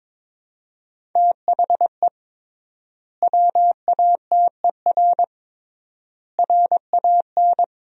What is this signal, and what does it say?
Morse code, heard as a keyed tone, about 22 wpm, 700 Hz. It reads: THE WATER RAN
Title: 3x Word-Spacing - Sentences from Top 500 Words - 22wpm